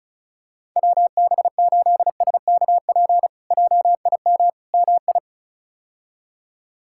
Morse code, heard as a keyed tone, about 36 wpm, 700 Hz. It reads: WB8SKP JIM MI
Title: K1USN SST QSO Practice - 35wpm